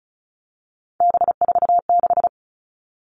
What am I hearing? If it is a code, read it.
Morse code, heard as a keyed tone, about 34 wpm, 700 Hz. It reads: B46
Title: All Cut Numbers - 3-Digit 35wpm